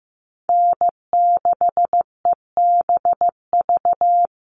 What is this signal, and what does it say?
Morse code, heard as a keyed tone, about 15 wpm, 700 Hz. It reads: N6EBV
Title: USA Call Signs Repeated 3x 15wpm